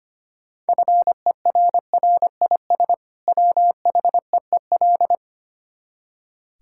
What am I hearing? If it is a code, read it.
Morse code, heard as a keyed tone, about 25 wpm, 690 Hz. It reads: FERRIS WHEEL
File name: practice / 25wpm / Easy Words with Hints - 25wpm